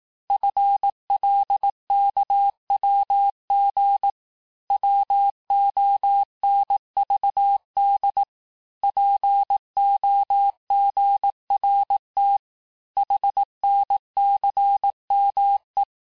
Morse code, heard as a keyed tone, about 18 wpm, 800 Hz. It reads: FLKWG WONVD POGRT HNCME
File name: cours-du-lundi files / cours-1594--teste-chiffre-vitesse-17mpm